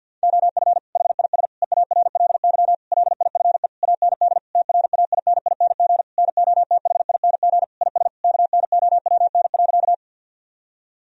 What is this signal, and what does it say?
Morse code, heard as a keyed tone, about 50 wpm, 700 Hz. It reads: OF HIS EARLY LIFE AND TRAINING NOTHING IS KNOWN.
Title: Sentences from Top 1000 Words - Review Part 4 - 50wpm